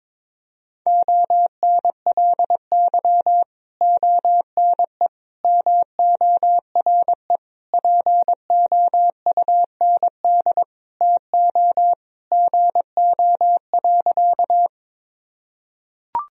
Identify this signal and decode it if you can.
Morse code, heard as a keyed tone, about 22 wpm, 700 Hz. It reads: ONLY ONE MORE POUND TO GO.
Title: Sentences from Top 1000 Words - Review Part 2 - 22wpm